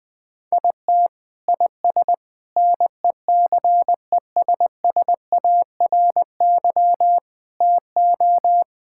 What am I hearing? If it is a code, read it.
Morse code, heard as a keyed tone, about 20 wpm, 700 Hz. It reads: IT IS NECESSARY TO
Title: Sentences from Top 1000 Words - Review Part 5 - 20wpm